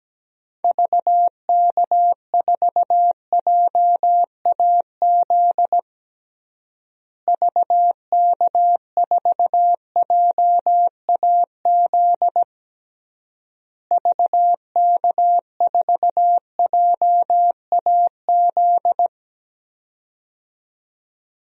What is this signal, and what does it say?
Morse code, heard as a keyed tone, about 17 wpm, 700 Hz. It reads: VK4JAZ VK4JAZ VK4JAZ
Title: DXCC-100 Call Signs - 6 Characters - Repeated 3x - 17wpm